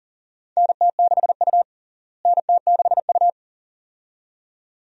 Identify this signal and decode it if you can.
Morse code, heard as a keyed tone, about 40 wpm, 700 Hz. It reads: NT6U NT6U
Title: USA Call Signs - 4 Characters - Repeated 2x - 40wpm